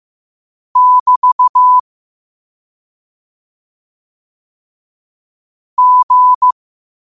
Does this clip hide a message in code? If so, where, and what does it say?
Yes: Morse code from 0.8 to 6.5 s, heard as a keyed tone, about 15 wpm, 1000 Hz. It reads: <BT> G